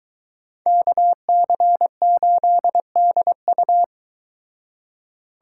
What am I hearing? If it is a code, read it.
Morse code, heard as a keyed tone, about 23 wpm, 700 Hz. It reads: KC8DU